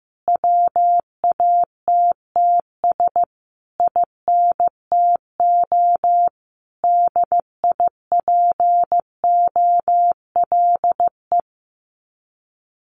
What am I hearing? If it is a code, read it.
Morse code, heard as a keyed tone, about 15 wpm, 700 Hz. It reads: WATTS INTO DIPOLE
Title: Common QSO Phrases Encore 15wpm